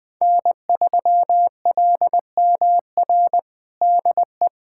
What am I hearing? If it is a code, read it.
Morse code, heard as a keyed tone, about 20 wpm, 700 Hz. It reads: N3LMR DE
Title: USA Callsign DE Callsign 20wpm